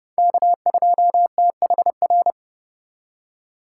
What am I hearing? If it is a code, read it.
Morse code, heard as a keyed tone, about 30 wpm, 700 Hz. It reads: K2THR
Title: USA Call Signs - 5 Characters - 2x Character Spacing - 30wpm